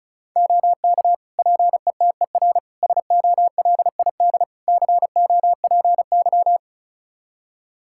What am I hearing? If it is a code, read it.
Morse code, heard as a keyed tone, about 34 wpm, 700 Hz. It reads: OK PETER SOLID COPY
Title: K1USN SST QSO Practice 35wpm